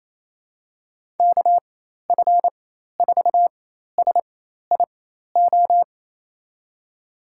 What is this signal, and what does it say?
Morse code, heard as a keyed tone, about 28 wpm, spaced as for 13 wpm, 700 Hz. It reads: KF4SIO
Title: USA Call Signs - 4x Character Spacing - 28wpm